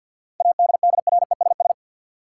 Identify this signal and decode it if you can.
Morse code, heard as a keyed tone, about 52 wpm, 700 Hz. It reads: ADDRESS
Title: Sets of 5 Words Encore - New Word Drill - 50wpm